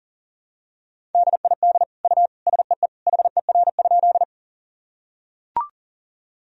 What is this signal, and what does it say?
Morse code, heard as a keyed tone, about 40 wpm, 700 Hz. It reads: DID U SEE HER?
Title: Sentences from Top 100 Words and Abbreviations 40wpm